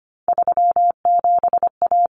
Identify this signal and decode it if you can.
Morse code, heard as a keyed tone, about 25 wpm, 700 Hz. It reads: 37A
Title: Common Cut Numbers - 3-Digit 25wpm